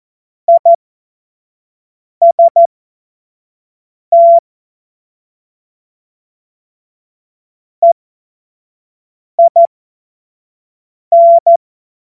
Morse code, heard as a keyed tone, about 14 wpm, spaced as for 5 wpm, 680 Hz. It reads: IST EIN
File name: Morsekurs / Witze / Witz 14. Dezember 2021, 25 F70